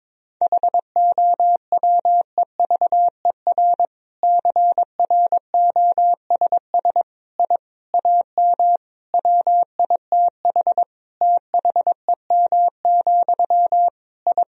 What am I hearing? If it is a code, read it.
Morse code, heard as a keyed tone, about 22 wpm, 700 Hz. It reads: HOWEVER CROSS I AM WITH THEM, I
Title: Sentences from Top 1000 Words - Review Part 1 - 22wpm